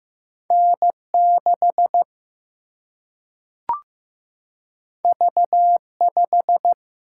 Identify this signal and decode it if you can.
Morse code, heard as a keyed tone, about 15 wpm, 700 Hz. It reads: N 6 V 5
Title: Single-Letter Single-Number 15wpm